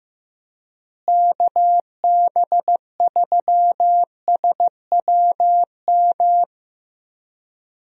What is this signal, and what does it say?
Morse code, heard as a keyed tone, about 15 wpm, 700 Hz. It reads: KB3SWM
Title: USA Call Signs - 6 Characters - Repeated 3x - 15wpm